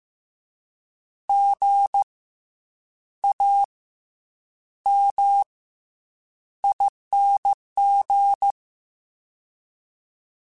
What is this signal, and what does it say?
Morse code, heard as a keyed tone, about 15 wpm, 800 Hz. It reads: G A M ING